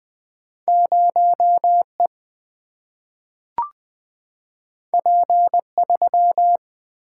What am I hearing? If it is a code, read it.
Morse code, heard as a keyed tone, about 20 wpm, 700 Hz. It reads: 0 E P 3
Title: Single-Letter Single-Number 20wpm